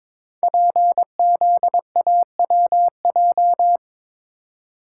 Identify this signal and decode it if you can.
Morse code, heard as a keyed tone, about 22 wpm, 700 Hz. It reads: PZAWJ
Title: Code Groups - 5 Letter:Numbers - 22wpm